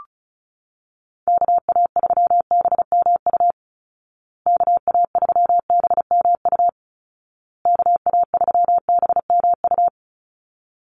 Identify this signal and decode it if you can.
Morse code, heard as a keyed tone, about 34 wpm, 700 Hz. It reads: KA3BMU KA3BMU KA3BMU
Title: Call Signs Repeated 2x - 35wpm